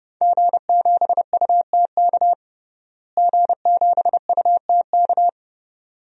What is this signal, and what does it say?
Morse code, heard as a keyed tone, about 32 wpm, 700 Hz. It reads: G7UTK G7UTK
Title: DXCC-100 Call Signs - 5 Characters - Repeated 2x - 30wpm